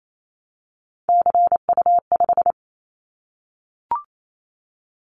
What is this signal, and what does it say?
Morse code, heard as a keyed tone, about 28 wpm, 700 Hz. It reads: CU5
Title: Callsign Prefixes for Top 50 DXCC - 28wpm